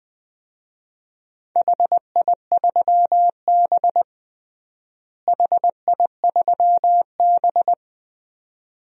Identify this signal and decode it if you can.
Morse code, heard as a keyed tone, about 21 wpm, 700 Hz. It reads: HI3B HI3B
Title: DXCC-100 Call Signs - 4 Characters - Repeated 2x - 20wpm